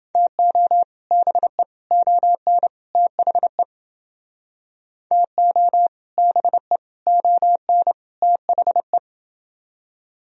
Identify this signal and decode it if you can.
Morse code, heard as a keyed tone, about 29 wpm, 700 Hz. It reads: TO BE ON THE TO BE ON THE